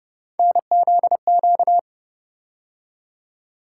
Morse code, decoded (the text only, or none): NZQ